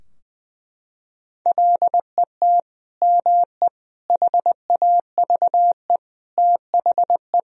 Morse code, heard as a keyed tone, about 20 wpm, 700 Hz. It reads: LET ME HAVE THE
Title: Sentences from Top 1000 Words - Review Part 2 - 20wpm